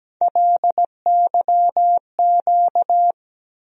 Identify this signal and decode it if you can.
Morse code, heard as a keyed tone, about 17 wpm, 700 Hz. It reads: LYQ
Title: USA Call Sign Postfixes - 3 Letters 17wpm